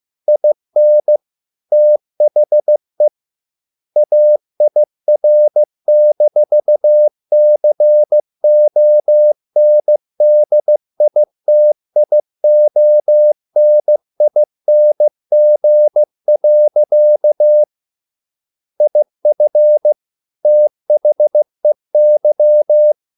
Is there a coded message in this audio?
Yes: Morse code from 0.3 to 22.9 s, heard as a keyed tone, about 15 wpm, 600 Hz. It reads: IN THE AIR-CONDITIONING. IF THEY